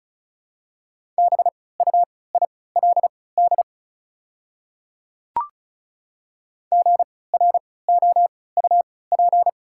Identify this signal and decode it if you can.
Morse code, heard as a keyed tone, about 34 wpm, spaced as for 20 wpm, 690 Hz. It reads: BUILD GROUP
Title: 3x Character Spacing - Top 500 Words - 35wpm